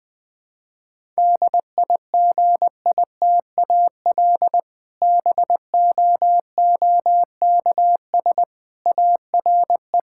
Morse code, heard as a keyed tone, about 20 wpm, 700 Hz. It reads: DIGITAL BOOKS ARE
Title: Difficult Sentences with Hints - 20wpm